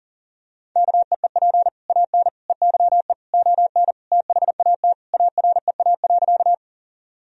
Morse code, heard as a keyed tone, about 39 wpm, 700 Hz. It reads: KEEP AN EYE ON THAT AREA.
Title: Sentences from Top 1000 Words - Review Part 2 - 40wpm